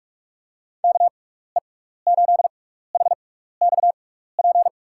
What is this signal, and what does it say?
Morse code, heard as a keyed tone, about 46 wpm, 700 Hz. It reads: K E 8 H X P
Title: USA Call Signs - 6 Characters - 6x Character Spacing - 45wpm